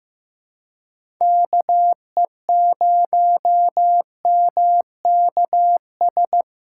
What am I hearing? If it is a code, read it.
Morse code, heard as a keyed tone, about 15 wpm, 700 Hz. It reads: KE0MKS
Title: USA Callsign DE Callsign - 6 Characters - 15wpm